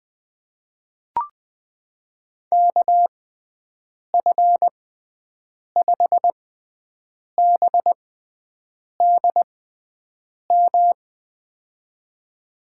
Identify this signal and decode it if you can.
Morse code, heard as a keyed tone, about 20 wpm, spaced as for 7 wpm, 700 Hz. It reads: KF5BDM
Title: USA Call Signs - 6 Characters - 6x Character Spacing - 20wpm